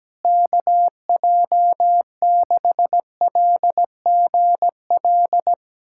Morse code, heard as a keyed tone, about 17 wpm, 700 Hz. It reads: KJ6LGL
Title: USA Callsign DE Callsign - 6 Characters - 17wpm